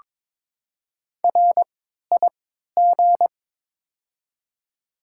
Morse code, heard as a keyed tone, about 22 wpm, spaced as for 13 wpm, 700 Hz. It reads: RIG
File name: Top 20 QSO Elements - 3x Character Spacing - 22wpm